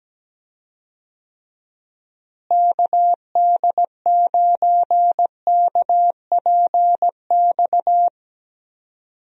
Morse code, heard as a keyed tone, about 17 wpm, 700 Hz. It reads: KD9KPX
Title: USA Call Signs - 6 Characters - Repeated 3x - 17wpm